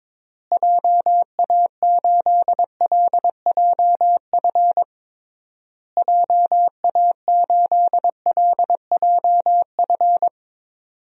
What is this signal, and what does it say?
Morse code, heard as a keyed tone, about 22 wpm, 700 Hz. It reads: JA8LJF JA8LJF
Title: DXCC-100 Call Signs - 6 Characters - Repeated 3x - 22wpm